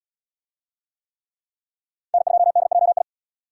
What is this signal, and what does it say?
Morse code, heard as a keyed tone, about 65 wpm, 700 Hz. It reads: N2DJI